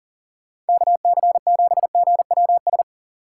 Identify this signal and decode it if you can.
Morse code, heard as a keyed tone, about 48 wpm, 700 Hz. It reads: KC7GWS
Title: Speed Racing - USA 2x3 Call Signs 40wpm